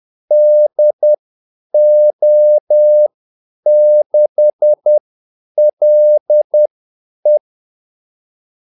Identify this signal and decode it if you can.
Morse code, heard as a keyed tone, about 10 wpm, spaced as for 8 wpm, 600 Hz. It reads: DO6LE
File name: Callsigns-10-8wpm